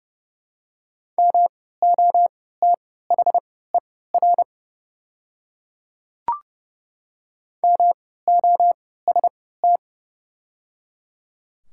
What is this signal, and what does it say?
Morse code, heard as a keyed tone, about 30 wpm, spaced as for 17 wpm, 700 Hz. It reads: MOTHER MOST
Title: 3x Character Spacing - Top 300 Words - 30wpm